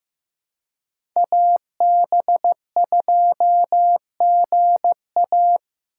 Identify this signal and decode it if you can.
Morse code, heard as a keyed tone, about 15 wpm, 700 Hz. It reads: AB2GA
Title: USA Call Signs - 5 Characters - 2x Character Spacing - 15wpm